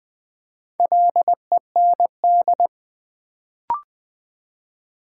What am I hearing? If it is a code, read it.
Morse code, heard as a keyed tone, about 20 wpm, 700 Hz. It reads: LEND
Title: Build A Word - 2 Letters At A Time - 4-Letter Words - 20wpm